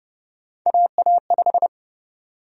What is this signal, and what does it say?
Morse code, heard as a keyed tone, about 29 wpm, 700 Hz. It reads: AA5
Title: USA Call Sign Prefixes - 30wpm